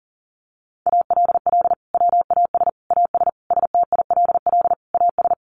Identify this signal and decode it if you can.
Morse code, heard as a keyed tone, about 40 wpm, 700 Hz. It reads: ALL WAS AS STILL AS
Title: Sentences from Top 800 Words 40wpm